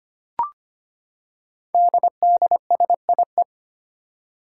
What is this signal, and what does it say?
Morse code, heard as a keyed tone, about 25 wpm, 700 Hz. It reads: DDSIE
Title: Five Challenging Letter-Number Combinations 25wpm